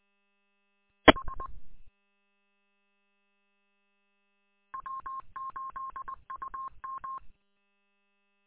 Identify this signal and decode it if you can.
Morse code, heard as a keyed tone, about 20 wpm, 1070 Hz. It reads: H W8UM
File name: w8um- -combined